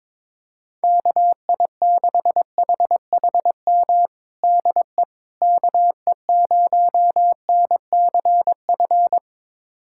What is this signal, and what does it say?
Morse code, heard as a keyed tone, about 21 wpm, 700 Hz. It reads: KI6HHM DE KE0NCF
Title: USA Callsign DE Callsign 22wpm